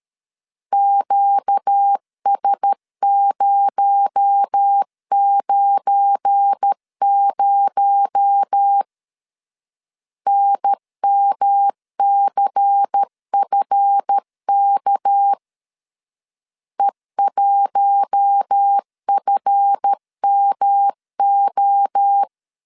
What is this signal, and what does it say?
Morse code, heard as a keyed tone, about 12 wpm, 800 Hz. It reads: QS090 NMCFK E1FMO
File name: LEZIONI / 7 GF9Z / 7 GF9 12wpm 2319